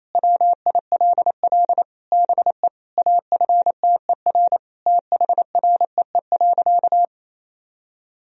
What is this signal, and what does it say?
Morse code, heard as a keyed tone, about 28 wpm, 700 Hz. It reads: WILL BE AFTER THREE.
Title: Sentences from Top 1000 Words - Review Part 1 - 28wpm